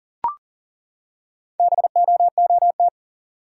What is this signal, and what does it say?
Morse code, heard as a keyed tone, about 39 wpm, 700 Hz. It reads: BOOT
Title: Four Letter Words - 40wpm